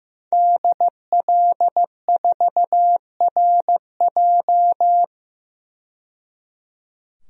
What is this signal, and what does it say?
Morse code, heard as a keyed tone, about 15 wpm, 700 Hz. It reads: DL4RJ
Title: DXCC-100 Call Signs - 5 Characters - 15wpm